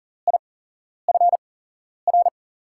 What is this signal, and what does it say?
Morse code, heard as a keyed tone, about 40 wpm, spaced as for 11 wpm, 700 Hz. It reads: IFR
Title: USA Call Signs - 5 Characters - 8x Character Spacing - 40wpm